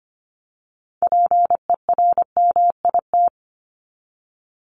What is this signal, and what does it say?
Morse code, heard as a keyed tone, about 25 wpm, 700 Hz. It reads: PERMIT
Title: Top 1751-2000 Words 25wpm